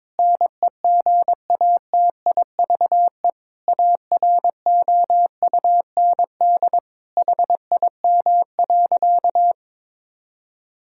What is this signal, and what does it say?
Morse code, heard as a keyed tone, about 23 wpm, 700 Hz. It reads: NEGATIVE AROUND HIM.
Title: Sentences from Top 1000 Words - Review Part 6 - 22wpm